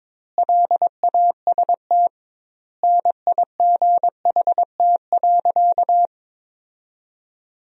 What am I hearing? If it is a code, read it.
Morse code, heard as a keyed tone, about 22 wpm, 700 Hz. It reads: LAST NIGHT.
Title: 2x Word-Spacing - Sentences from Top 300 Words - 22wpm